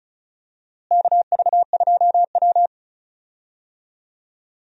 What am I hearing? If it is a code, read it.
Morse code, heard as a keyed tone, about 36 wpm, 700 Hz. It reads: KV2W
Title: USA Call Signs - 4 Characters - Repeated 2x - 35wpm